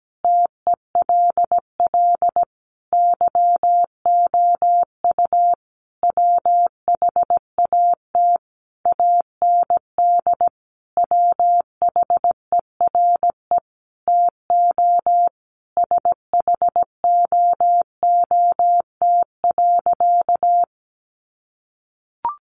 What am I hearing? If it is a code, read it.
Morse code, heard as a keyed tone, about 17 wpm, 700 Hz. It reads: TELL YOU WHAT AND WHERE TO SHOOT.
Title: A Sound Of Thunder 17wpm